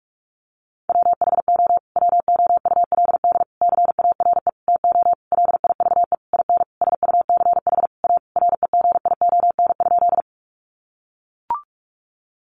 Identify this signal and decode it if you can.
Morse code, heard as a keyed tone, about 44 wpm, 700 Hz. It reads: WHO WOULD CARE TO LIVE IN SUCH A REGION?
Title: Sentences from Top 1500 Words 45wpm